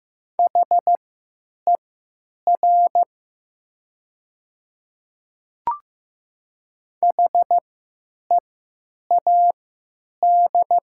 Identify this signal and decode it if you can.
Morse code, heard as a keyed tone, about 16 wpm, spaced as for 9 wpm, 700 Hz. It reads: HER HEAD
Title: 3x Character Spacing - Top 300 Words - 15wpm